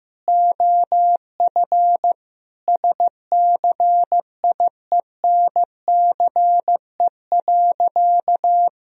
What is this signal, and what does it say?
Morse code, heard as a keyed tone, about 15 wpm, 700 Hz. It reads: OF SCIENCE.
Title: Sentences from Top 1000 Words - Review Part 2 - 15wpm